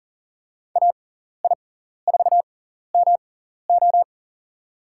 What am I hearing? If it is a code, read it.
Morse code, heard as a keyed tone, about 40 wpm, spaced as for 14 wpm, 700 Hz. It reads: AI4MO
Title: USA Call Signs - 6x Character Spacing - 40wpm